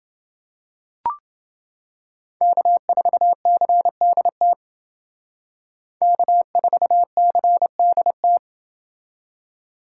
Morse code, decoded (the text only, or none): K4CDT K4CDT